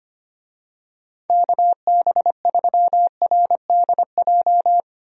Morse code, heard as a keyed tone, about 25 wpm, 700 Hz. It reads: KB3RDJ
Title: USA Call Signs - 6 Characters - Repeated 2x - 25wpm